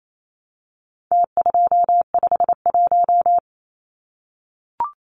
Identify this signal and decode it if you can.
Morse code, heard as a keyed tone, about 28 wpm, 700 Hz. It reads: T251
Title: Numbers - Contest Style - 4-Digit 28wpm